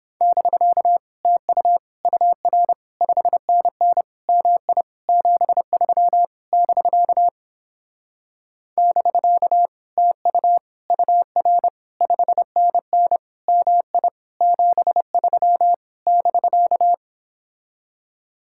Morse code, heard as a keyed tone, about 29 wpm, 700 Hz. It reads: <BK> TU UR 5NN MI 73 <BK> <BK> TU UR 5NN MI 73 <BK>